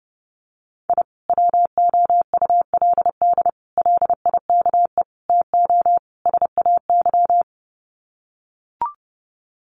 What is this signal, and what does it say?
Morse code, heard as a keyed tone, about 30 wpm, 700 Hz. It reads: I WOULD LIKE TO SAY
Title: Sets of 5 Words Encore2 30wpm